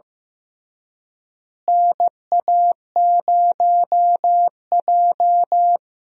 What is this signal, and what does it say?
Morse code, heard as a keyed tone, about 15 wpm, 700 Hz. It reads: NA0J
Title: USA Call Signs - 4 Characters - Repeated 2x - 15wpm